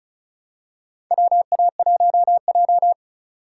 Morse code, heard as a keyed tone, about 36 wpm, 700 Hz. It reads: WA1J
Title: USA Call Signs - 4 Characters - 2x Character Spacing - 35wpm